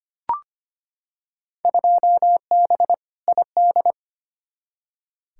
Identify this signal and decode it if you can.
Morse code, heard as a keyed tone, about 25 wpm, 700 Hz. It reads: 2B ID
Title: Field Day Exchanges Encore - 25wpm